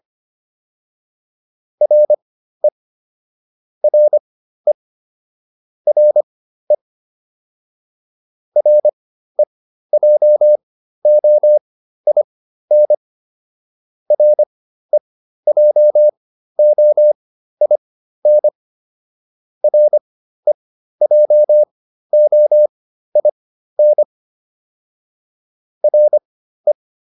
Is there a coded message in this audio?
Yes: Morse code from 1.8 to 26.7 s, heard as a keyed tone, about 25 wpm, spaced as for 13 wpm, 600 Hz. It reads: RE RE RE REJOIN REJOIN REJOIN RE